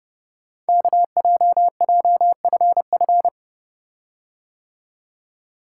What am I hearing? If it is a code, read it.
Morse code, heard as a keyed tone, about 30 wpm, 700 Hz. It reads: KJJFF
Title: Code Groups - 5 Letters - 30wpm